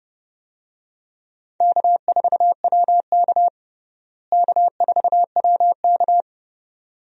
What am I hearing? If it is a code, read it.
Morse code, heard as a keyed tone, about 30 wpm, 700 Hz. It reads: K4WK K4WK
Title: USA Call Signs - 4 Characters - Repeated 2x - 30wpm